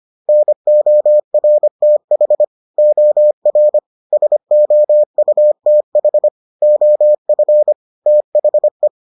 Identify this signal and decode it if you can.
Morse code, heard as a keyed tone, about 25 wpm, 600 Hz. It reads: NORTH OR SOUTH OF THE